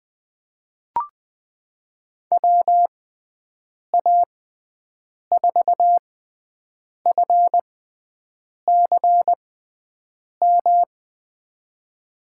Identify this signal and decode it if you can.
Morse code, heard as a keyed tone, about 20 wpm, spaced as for 7 wpm, 700 Hz. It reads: WA4FCM
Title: USA Call Signs - 6 Characters - 6x Character Spacing - 20wpm